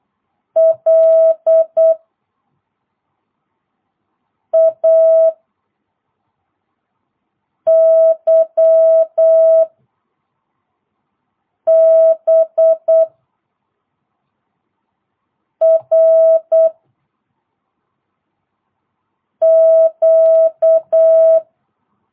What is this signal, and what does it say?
Morse code, heard as a keyed tone, about 8 wpm, 660 Hz. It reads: L A Y B R Q